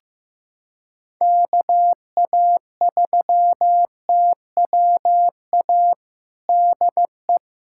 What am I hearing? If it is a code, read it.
Morse code, heard as a keyed tone, about 15 wpm, 700 Hz. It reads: KA3TWA DE
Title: USA Callsign DE Callsign - 6 Characters - 15wpm